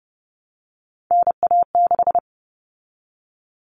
Morse code, heard as a keyed tone, about 30 wpm, 700 Hz. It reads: NA6
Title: Common Cut Numbers - 3-Digit 30wpm